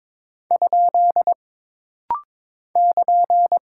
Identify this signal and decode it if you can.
Morse code, heard as a keyed tone, about 22 wpm, 700 Hz. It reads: ? <KN>
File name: practice / 22wpm / Single Punctuation-Prosign - Rapid-Fire - 22wpm